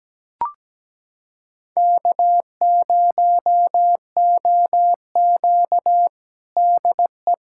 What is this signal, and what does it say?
Morse code, heard as a keyed tone, about 17 wpm, 700 Hz. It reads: K0OQ DE
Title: USA Callsign DE Callsign - 4 Characters - 17wpm